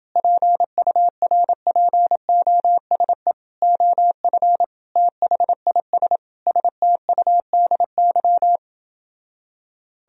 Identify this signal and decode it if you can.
Morse code, heard as a keyed tone, about 29 wpm, 700 Hz. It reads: PURPOSE OF THIS STUDY